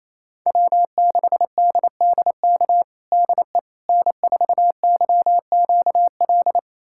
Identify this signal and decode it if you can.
Morse code, heard as a keyed tone, about 28 wpm, 700 Hz. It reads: W6DDK DE N4YQL
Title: USA Callsign DE Callsign - 5 Characters - 28wpm